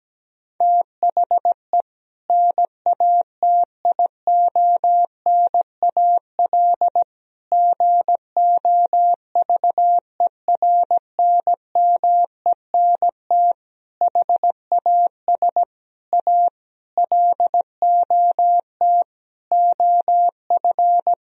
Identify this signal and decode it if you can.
Morse code, heard as a keyed tone, about 17 wpm, 700 Hz. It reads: THE NATIONAL GOVERNMENT HAS A LOT OF
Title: Sentences from Top 1000 Words - Review Part 4 - 17wpm